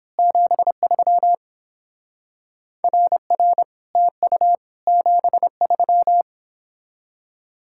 Morse code, heard as a keyed tone, about 26 wpm, 700 Hz. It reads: 73 RR TU 73
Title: Speed-Racing Common QSO Phrases Encore - 22wpm